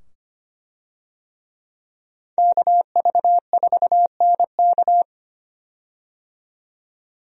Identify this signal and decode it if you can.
Morse code, heard as a keyed tone, about 25 wpm, 700 Hz. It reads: KV4NK
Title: USA Call Signs - 5 Characters - 25wpm